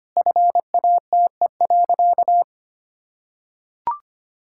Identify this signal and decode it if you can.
Morse code, heard as a keyed tone, about 25 wpm, 700 Hz. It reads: FATE.
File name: practice / 25wpm / Sentences with 4-Word Phrases - 25wpm